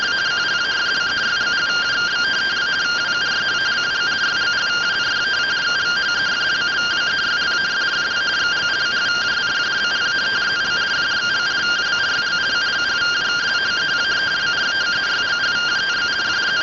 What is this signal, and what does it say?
Two-tone teleprinter signal, about 170 Hz shift.
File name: findings / CT 1 Scout 0001.gen-42 rtty rx